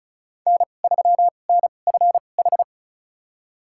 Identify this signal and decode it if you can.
Morse code, heard as a keyed tone, about 36 wpm, spaced as for 25 wpm, 700 Hz. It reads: N3NFH
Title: USA Call Signs - 5 Characters - 2x Character Spacing - 35wpm